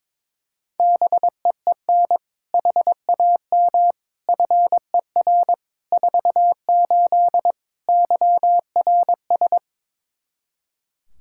Morse code, heard as a keyed tone, about 22 wpm, 700 Hz. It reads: BEEN HAM FER 48 YRS
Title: Practice QSO - Exchange 3 - More Info - 22wpm